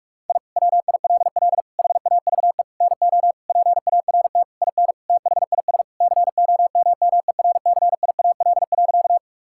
Morse code, heard as a keyed tone, about 42 wpm, 700 Hz. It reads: I WILL HAVE NO PART IN THIS COMMERCIAL.
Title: Sentences from Top 1000 Words - Review Part 5 - 45wpm